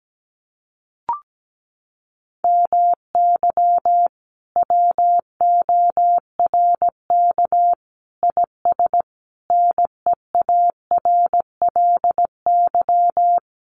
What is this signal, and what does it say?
Morse code, heard as a keyed tone, about 17 wpm, 700 Hz. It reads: MY WORK IS NEARLY
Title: Sentences from Top 1300 Words 17wpm